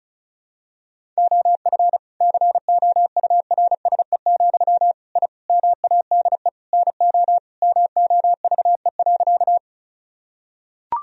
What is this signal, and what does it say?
Morse code, heard as a keyed tone, about 34 wpm, 700 Hz. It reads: OF COURSE, I MADE NO MOVE.
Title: Sentences from Top 1000 Words - Review Part 1 - 35wpm